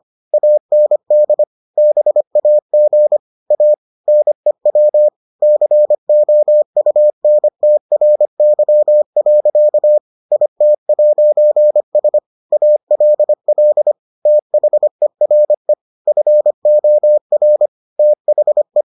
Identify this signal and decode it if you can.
Morse code, heard as a keyed tone, about 25 wpm, 600 Hz. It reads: AND BAG A NEW COUNTRY. IT'S ALL THERE FOR THE